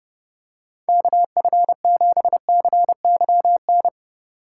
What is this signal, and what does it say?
Morse code, heard as a keyed tone, about 32 wpm, 700 Hz. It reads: KF7CYN
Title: USA Call Signs - 6 Characters - 3x Character Spacing - 30wpm